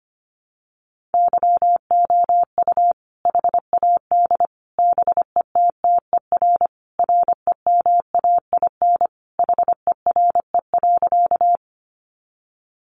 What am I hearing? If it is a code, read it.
Morse code, heard as a keyed tone, about 25 wpm, 700 Hz. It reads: YOU HAD BETTER REMAIN HERE.
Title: Sentences from Top 1200 Words 25wpm